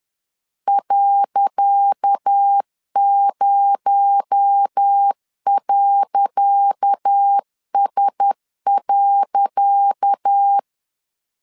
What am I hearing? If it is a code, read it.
Morse code, heard as a keyed tone, about 10 wpm, 800 Hz. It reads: .0.S.
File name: LEZIONI / 10 SIGNS / 10C DOT 10wpm 2940